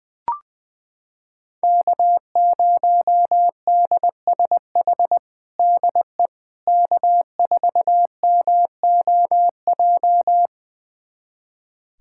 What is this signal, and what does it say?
Morse code, heard as a keyed tone, about 21 wpm, 700 Hz. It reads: K0DSH DE K4MOJ
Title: USA Callsign DE Callsign - 5 Characters - 20wpm